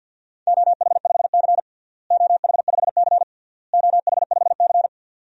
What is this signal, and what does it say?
Morse code, heard as a keyed tone, about 52 wpm, 700 Hz. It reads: OH5C OH5C OH5C
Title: DXCC-100 Call Signs - 4 Characters - Repeated 3x - 50wpm